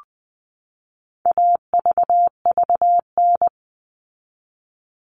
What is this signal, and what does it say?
Morse code, heard as a keyed tone, about 20 wpm, 700 Hz. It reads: AVVN
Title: All Cut Numbers - 4-Digit 20wpm